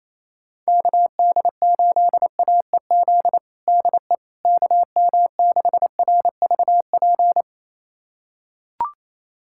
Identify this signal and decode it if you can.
Morse code, heard as a keyed tone, about 28 wpm, 700 Hz. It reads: KD8AEZ DE KM6RVP
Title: USA Callsign DE Callsign - 6 Characters - 28wpm